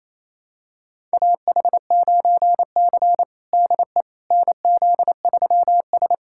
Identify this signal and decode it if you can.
Morse code, heard as a keyed tone, about 27 wpm, 700 Hz. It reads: AH9C DE NZ3S
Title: USA Callsign DE Callsign - 4 Characters - 28wpm